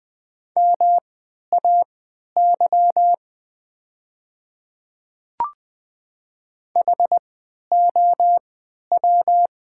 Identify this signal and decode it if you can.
Morse code, heard as a keyed tone, about 20 wpm, spaced as for 11 wpm, 700 Hz. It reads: MAY HOW
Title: 3x Character Spacing - Top 200 Words - 20wpm